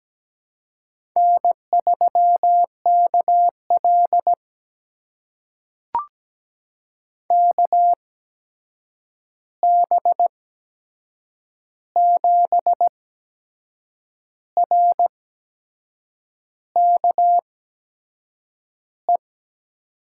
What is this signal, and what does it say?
Morse code, heard as a keyed tone, about 17 wpm, spaced as for 6 wpm, 700 Hz. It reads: N3KL KB7RKE
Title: USA Call Signs - 8x Character Spacing - 17wpm